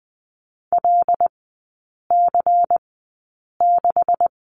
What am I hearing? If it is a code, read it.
Morse code, heard as a keyed tone, about 20 wpm, spaced as for 8 wpm, 700 Hz. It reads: LC6
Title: 127 Lesson 21e - Number 6 - Three Characters 20wpm